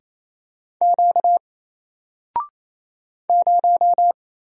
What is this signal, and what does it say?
Morse code, heard as a keyed tone, about 27 wpm, 700 Hz. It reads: Q 0
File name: Single-Letter-Number-28wpm